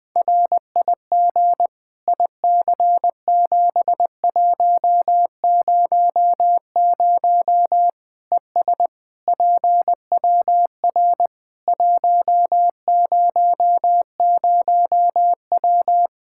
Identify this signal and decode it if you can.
Morse code, heard as a keyed tone, about 21 wpm, 700 Hz. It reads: RIG IC7100 ES PWR 100W
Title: Practice QSO - Exchange 2 - Info - 20wpm